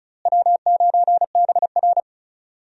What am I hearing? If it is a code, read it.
Morse code, heard as a keyed tone, about 36 wpm, 700 Hz. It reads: W9BR
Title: USA Call Signs - 4 Characters - Repeated 3x - 35wpm